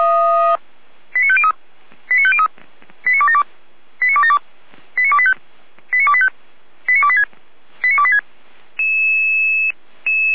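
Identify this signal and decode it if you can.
Five-tone selective call (ZVEI), heard as quick runs of five tones, tones of about 70 ms. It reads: 89503 89503 89272 89272 89285 89285 89286 89286
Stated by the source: ZVEI five-tone selective call 89270 received by SDR.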